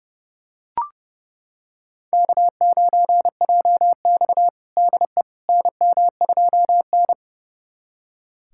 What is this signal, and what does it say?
Morse code, heard as a keyed tone, about 30 wpm, 700 Hz. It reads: K9JX DE NM2N
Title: USA Callsign DE Callsign - 4 Characters - 30wpm